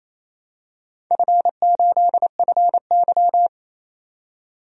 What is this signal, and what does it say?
Morse code, heard as a keyed tone, about 28 wpm, 700 Hz. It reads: F8FY
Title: DXCC-100 Call Signs - 4 Characters - 28wpm